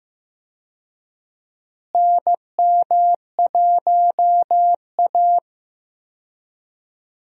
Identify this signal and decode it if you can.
Morse code, heard as a keyed tone, about 15 wpm, 700 Hz. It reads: NM1A
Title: USA Call Signs - 4 Characters - Repeated 2x - 15wpm